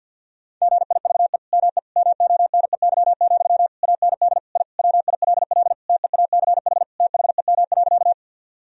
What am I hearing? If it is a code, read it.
Morse code, heard as a keyed tone, about 50 wpm, 700 Hz. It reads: GIVE ME MONEY, AND I WILL TEACH THEM.
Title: Sentences from Top 1000 Words - Review Part 2 - 50wpm